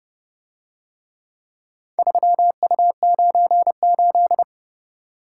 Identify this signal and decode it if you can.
Morse code, heard as a keyed tone, about 27 wpm, 700 Hz. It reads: 3U98